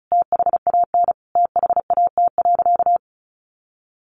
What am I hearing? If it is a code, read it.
Morse code, heard as a keyed tone, about 34 wpm, 700 Hz. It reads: THAN THAT.
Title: Sentences from Top 800 Words 35wpm